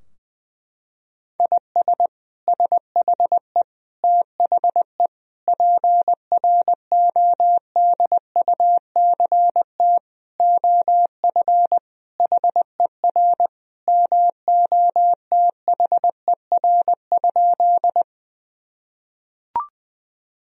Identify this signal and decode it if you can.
Morse code, heard as a keyed tone, about 20 wpm, 700 Hz. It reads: IS SHE THE PRODUCT OF HER MOTHER?
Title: Sentences from Top 1000 Words - Review Part 2 - 20wpm